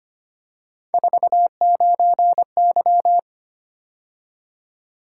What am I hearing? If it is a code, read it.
Morse code, heard as a keyed tone, about 25 wpm, 700 Hz. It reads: 49Y
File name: Three Letter-Number Combinations - 25wpm